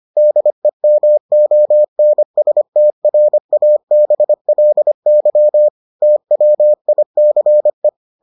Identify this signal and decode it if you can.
Morse code, heard as a keyed tone, about 25 wpm, 600 Hz. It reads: DEMONSTRABLY TWICE